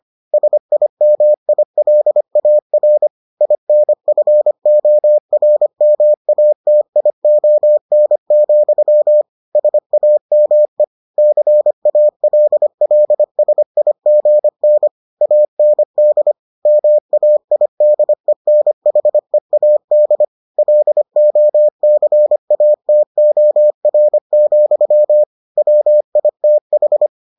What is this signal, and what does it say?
Morse code, heard as a keyed tone, about 25 wpm, 600 Hz. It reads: SIMILAR INFORMATION, SAME CALLSIGN AND MAIDENHEAD LOCATOR, WITH